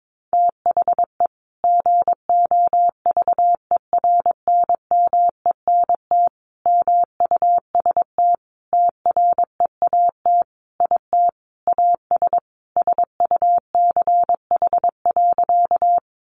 Morse code, heard as a keyed tone, about 22 wpm, 700 Hz. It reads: THE GOVERNMENT MUST TREAT IT AS SUCH.
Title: Sentences from Top 1300 Words 22wpm